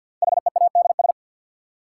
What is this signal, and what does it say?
Morse code, heard as a keyed tone, about 53 wpm, 700 Hz. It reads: HEADS
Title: Sentences from Top 1000 Words - Review Part 3 - New Word Drill - 50wpm